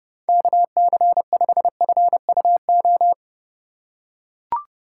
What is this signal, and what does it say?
Morse code, heard as a keyed tone, about 30 wpm, 700 Hz. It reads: KC5FUO
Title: USA Call Signs - 4x Character Spacing - 30wpm